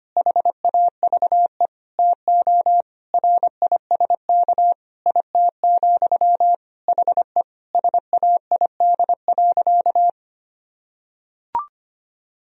Morse code, heard as a keyed tone, about 25 wpm, 700 Hz. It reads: HAVE TO RISK IT, HE SAID.
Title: Sentences from Top 1000 Words - Review Part 4 - 25wpm